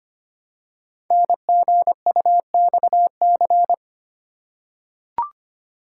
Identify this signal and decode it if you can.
Morse code, heard as a keyed tone, about 25 wpm, 700 Hz. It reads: NGUXC
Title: Code Groups - 5 Letter:Numbers - 25wpm